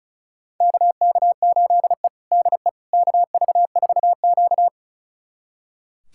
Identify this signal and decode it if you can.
Morse code, heard as a keyed tone, about 36 wpm, 700 Hz. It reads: KK8E DE KV4Q
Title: USA Callsign DE Callsign - 4 Characters - 35wpm